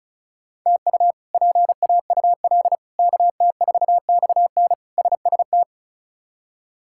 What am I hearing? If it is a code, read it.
Morse code, heard as a keyed tone, about 36 wpm, 700 Hz. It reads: TU PAUL KT4XN SST
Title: K1USN SST QSO Practice - 35wpm